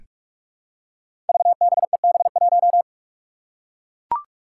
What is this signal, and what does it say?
Morse code, heard as a keyed tone, about 44 wpm, 700 Hz. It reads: VBEB1
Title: Five Challenging Letter-Number Combinations 45wpm